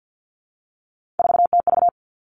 Morse code, heard as a keyed tone, about 50 wpm, 700 Hz. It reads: 4TV
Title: Three Challenging Letter-Number Combinations 50wpm